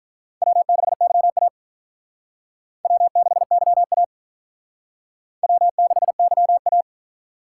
Morse code, heard as a keyed tone, about 46 wpm, 700 Hz. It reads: W6YA W6YA W6YA